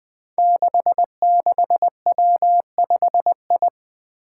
Morse code, heard as a keyed tone, about 20 wpm, 700 Hz. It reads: 66W5I
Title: Five Challenging Letter-Number Combinations 20wpm